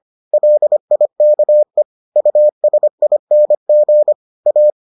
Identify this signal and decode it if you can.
Morse code, heard as a keyed tone, about 25 wpm, 600 Hz. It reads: LIKE USING A